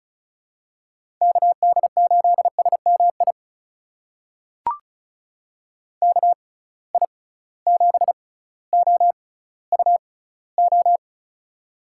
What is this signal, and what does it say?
Morse code, heard as a keyed tone, about 34 wpm, spaced as for 18 wpm, 700 Hz. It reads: KD8SMI KI7OUO
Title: USA Call Signs - 6 Characters - 6x Character Spacing - 35wpm